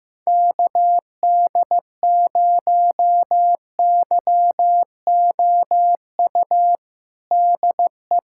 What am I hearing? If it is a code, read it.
Morse code, heard as a keyed tone, about 15 wpm, 700 Hz. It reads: KD0YOU DE
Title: USA Callsign DE Callsign - 6 Characters - 15wpm